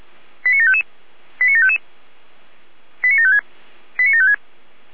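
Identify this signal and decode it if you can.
Five-tone selective call (ZVEI), heard as quick runs of five tones, tones of about 70 ms. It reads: 89755 89755 89756 89756
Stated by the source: ZVEI five-tone selective call received on an SDR by Feuerwehr Orlamünde.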